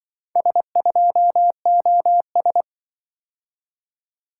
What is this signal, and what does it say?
Morse code, heard as a keyed tone, about 24 wpm, 700 Hz. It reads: S2OS
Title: Speed-Racing - Four Letter-Number Combinations - 20wpm